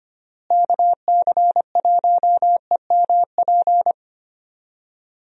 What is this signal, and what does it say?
Morse code, heard as a keyed tone, about 25 wpm, 700 Hz. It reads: KC1EMP
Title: USA Call Signs - 6 Characters - Repeated 2x - 25wpm